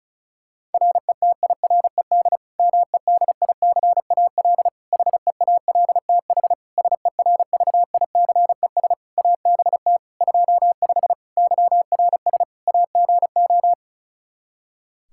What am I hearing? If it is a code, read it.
Morse code, heard as a keyed tone, about 34 wpm, 700 Hz. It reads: RETIRED MEDICAL HEALTH SERVICES ABT 25 YRS AGO
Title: Practice QSO - Exchange 3 - More Info - 35wpm